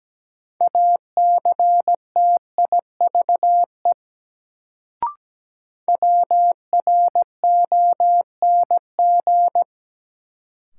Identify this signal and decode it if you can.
Morse code, heard as a keyed tone, about 17 wpm, 700 Hz. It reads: ACTIVE WRONG
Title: Sentences from Top 900 Words - New Word Drill - 17wpm